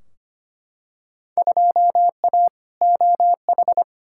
Morse code, heard as a keyed tone, about 25 wpm, 700 Hz. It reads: 2A OH
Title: Field Day Exchanges - 25wpm